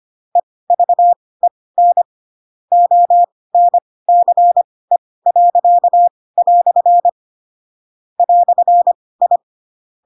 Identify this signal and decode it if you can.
Morse code, heard as a keyed tone, about 25 wpm, spaced as for 18 wpm, 700 Hz. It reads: EVEN ONCE." "I
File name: Pinocchio / Pinocchio ch4 12 25 ik1hge